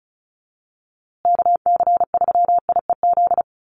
Morse code, heard as a keyed tone, about 34 wpm, 700 Hz. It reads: KC3IEZ
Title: USA Call Signs Repeated 3x 35wpm